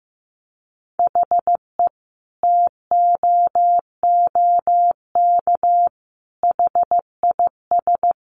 Morse code, heard as a keyed tone, about 15 wpm, 700 Hz. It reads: HE TOOK HIS
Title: Sentences from Top 900 Words 15wpm